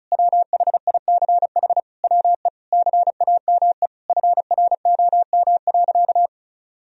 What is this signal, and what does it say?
Morse code, heard as a keyed tone, about 36 wpm, 700 Hz. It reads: WHICH WE CAME FROM.
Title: Sentences from Top 1000 Words - Review Part 4 - 35wpm